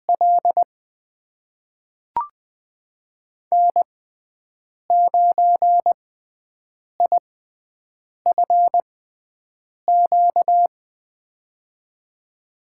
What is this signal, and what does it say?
Morse code, heard as a keyed tone, about 20 wpm, spaced as for 7 wpm, 700 Hz. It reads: L N9IFQ
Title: USA Call Signs - 5 Characters - 6x Character Spacing - 20wpm